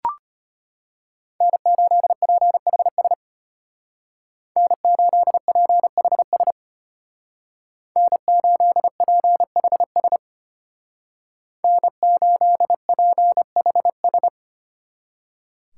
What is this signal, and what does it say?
Morse code, heard as a keyed tone, about 32 wpm, 700 Hz. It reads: N8PHS N8PHS N8PHS N8PHS